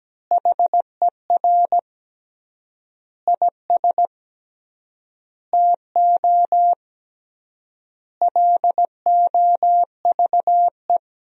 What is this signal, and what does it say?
Morse code, heard as a keyed tone, about 17 wpm, 700 Hz. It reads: HER IS TO LOVE
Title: 3x Word-Spacing - Sentences from Top 400 Words - 17wpm